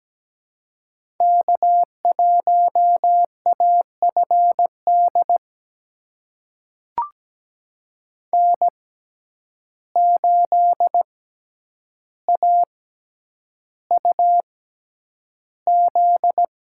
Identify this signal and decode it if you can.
Morse code, heard as a keyed tone, about 17 wpm, spaced as for 9 wpm, 700 Hz. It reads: K1AFD N8AUZ
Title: USA Call Signs - 5 Characters - 6x Character Spacing - 17wpm